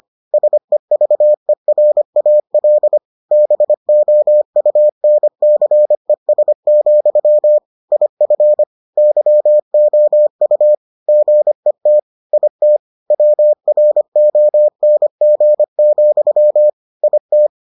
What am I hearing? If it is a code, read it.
Morse code, heard as a keyed tone, about 25 wpm, 600 Hz. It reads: SEVERAL BOUNCES, IF YOU GET IT WRONG, IT